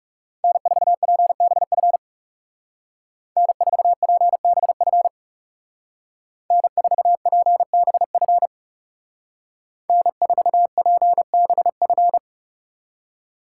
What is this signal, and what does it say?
Morse code, heard as a keyed tone, about 37 wpm, 700 Hz. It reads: N4PBF N4PBF N4PBF N4PBF
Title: Speed Racing - USA 1x3 Call Signs 30wpm